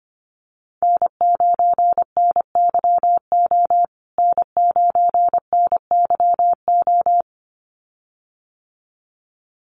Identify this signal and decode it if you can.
Morse code, heard as a keyed tone, about 25 wpm, 700 Hz. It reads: N9NYO N9NYO
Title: USA Call Signs Repeated 2x 25wpm